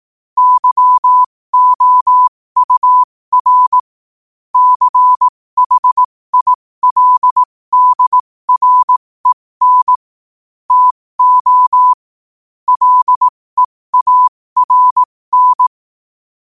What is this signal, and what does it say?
Morse code, heard as a keyed tone, about 18 wpm, spaced as for 15 wpm, 1000 Hz. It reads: YOUR CHILDREN TO LEARN
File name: qotdfiles / 15wpm